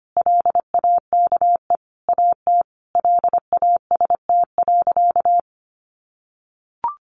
Morse code, heard as a keyed tone, about 25 wpm, 700 Hz. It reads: LAKE AT LAST.
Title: Sentences from Top 1100 Words 25wpm